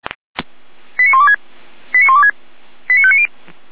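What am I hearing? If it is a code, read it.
Five-tone selective call (ZVEI), heard as quick runs of five tones, tones of about 70 ms. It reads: 89126 89126 89590
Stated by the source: Five-tone ZVEI selective call 89092, received by SDR.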